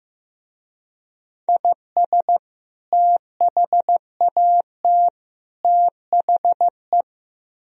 Morse code, heard as a keyed tone, about 17 wpm, 700 Hz. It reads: IS THAT THE
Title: Speed-Racing - Sets of 3 Words 15wpm